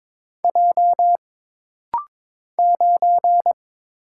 Morse code, heard as a keyed tone, about 22 wpm, 700 Hz. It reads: J 9
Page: This rapid-fire timing is geared to push your ICR (Instant Character Recognition) skills.